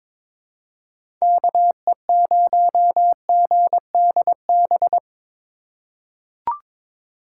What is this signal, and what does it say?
Morse code, heard as a keyed tone, about 22 wpm, 700 Hz. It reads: KE0GDB
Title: USA Call Signs - 6 Characters - 3x Character Spacing - 22wpm